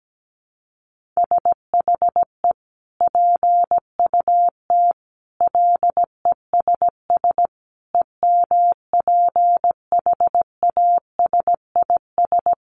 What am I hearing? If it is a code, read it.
Morse code, heard as a keyed tone, about 17 wpm, 700 Hz. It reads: SHE PUT LESS EMPHASIS
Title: Sentences from Top 1400 Words 17wpm